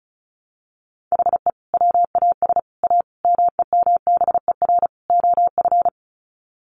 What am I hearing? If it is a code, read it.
Morse code, heard as a keyed tone, about 34 wpm, 700 Hz. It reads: HE WAS A MEMBER OF
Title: Sets of 5 Words Encore2 35wpm